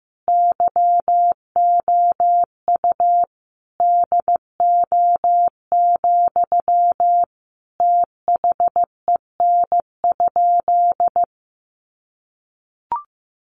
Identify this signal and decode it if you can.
Morse code, heard as a keyed tone, about 15 wpm, 700 Hz. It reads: YOU DO, THEN?
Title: Sentences from Top 100 Words 15wpm